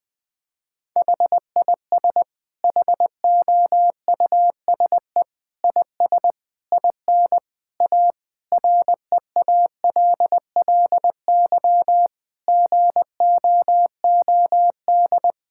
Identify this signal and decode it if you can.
Morse code, heard as a keyed tone, about 20 wpm, 700 Hz. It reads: HIS HOUSE IS IN A REALLY GOOD
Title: Sentences from Top 1000 Words - Review Part 6 - 20wpm